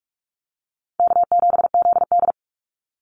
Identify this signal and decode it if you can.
Morse code, heard as a keyed tone, about 44 wpm, 700 Hz. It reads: K7ZD
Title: Easy Call Signs Repeated 3x - 45wpm